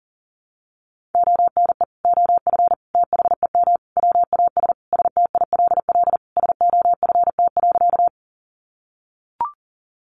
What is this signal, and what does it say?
Morse code, heard as a keyed tone, about 40 wpm, 700 Hz. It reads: ONE OF THEM WAS STILL SOFT.
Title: Sentences from Top 1500 Words 40wpm